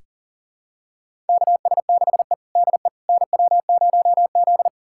Morse code, heard as a keyed tone, about 40 wpm, 690 Hz. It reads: KS6E DE NW0Z
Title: USA Callsign DE Callsign - 4 Characters - 40wpm